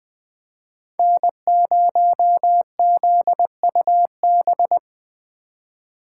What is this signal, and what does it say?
Morse code, heard as a keyed tone, about 21 wpm, 700 Hz. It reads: N0ZUB